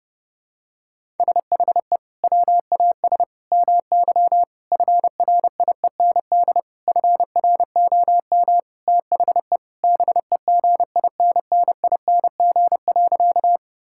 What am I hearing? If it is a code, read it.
Morse code, heard as a keyed tone, about 32 wpm, 700 Hz. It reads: SHE WAS MY FRIEND FROM THE BEGINNING.
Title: Sentences from Top 1000 Words - Review Part 5 - 30wpm